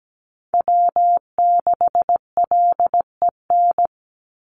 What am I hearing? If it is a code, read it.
Morse code, heard as a keyed tone, about 17 wpm, 700 Hz. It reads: W6LEN
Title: Call Signs - 17wpm